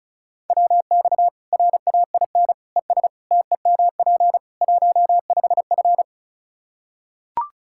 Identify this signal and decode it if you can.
Morse code, heard as a keyed tone, about 36 wpm, 700 Hz. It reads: WX RAIN ES TEMP 15F
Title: Practice QSO - Exchange 2 - Info - 35wpm